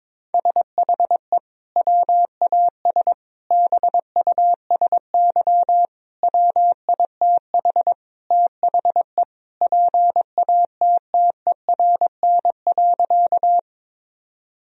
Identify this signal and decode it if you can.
Morse code, heard as a keyed tone, about 22 wpm, 700 Hz. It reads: SHE WAS BUSY WITH THE PATTERN.
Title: Sentences from Top 1000 Words - Review Part 2 - 22wpm